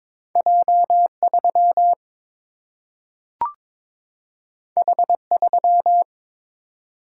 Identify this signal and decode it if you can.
Morse code, heard as a keyed tone, about 22 wpm, 700 Hz. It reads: J 3 H 3
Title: Single-Letter Single-Number 22wpm